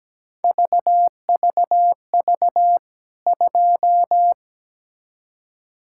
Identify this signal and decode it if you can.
Morse code, heard as a keyed tone, about 17 wpm, 700 Hz. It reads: VVV 2
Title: K1USN SST QSO Practice 17wpm